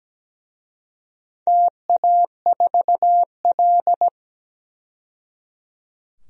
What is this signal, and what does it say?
Morse code, heard as a keyed tone, about 17 wpm, 700 Hz. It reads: TA4L
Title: DXCC-100 Call Signs - 4 Characters - 17wpm